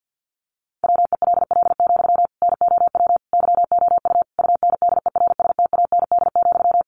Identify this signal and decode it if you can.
Morse code, heard as a keyed tone, about 48 wpm, 700 Hz. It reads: WELL, NOW YOU UNDERSTAND,
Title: Sentences from Top 600 Words 50wpm